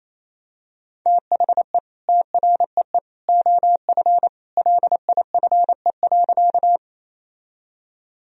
Morse code, heard as a keyed tone, about 28 wpm, 700 Hz. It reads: THE TREE OF LIFE.
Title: Sentences from Top 1000 Words - Review Part 1 - 28wpm